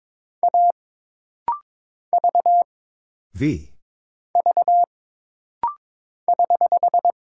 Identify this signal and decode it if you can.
Morse code, heard as a keyed tone, about 22 wpm, 700 Hz. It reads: A V V <HH>
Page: This rapid-fire timing is geared to push your ICR (Instant Character Recognition) skills.